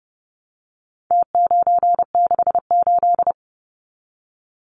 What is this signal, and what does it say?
Morse code, heard as a keyed tone, about 30 wpm, 700 Hz. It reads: T968
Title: Numbers - Contest Style - 4-Digit 30wpm